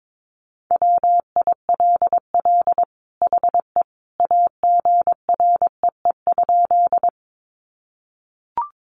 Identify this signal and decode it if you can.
Morse code, heard as a keyed tone, about 22 wpm, 700 Hz. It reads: WILL HE AGREE?
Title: Sentences from Top 1100 Words 22wpm